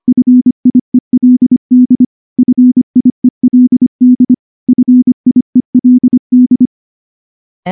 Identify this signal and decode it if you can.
Morse code, heard as a keyed tone, about 23 wpm, 250 Hz. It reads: FIELD FIELD FIELD
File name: morse / morse-training-25WPM